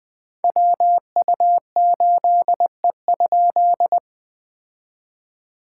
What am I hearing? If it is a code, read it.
Morse code, heard as a keyed tone, about 20 wpm, 700 Hz. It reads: WU8E?
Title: POTA QSO Practice - 20wpm